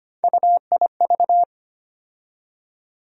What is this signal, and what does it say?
Morse code, heard as a keyed tone, about 25 wpm, 700 Hz. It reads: UIV
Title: USA Call Sign Postfixes - 3 Letters 25wpm